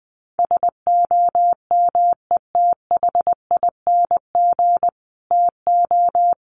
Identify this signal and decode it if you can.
Morse code, heard as a keyed tone, about 19 wpm, 700 Hz. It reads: SOMETHING TO
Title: Sets of 5 Words Encore 20wpm